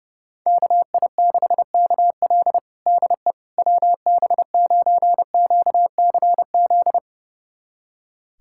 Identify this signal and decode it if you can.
Morse code, heard as a keyed tone, about 30 wpm, 700 Hz. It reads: KI6KL DE WB9QCZ
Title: USA Callsign DE Callsign 30wpm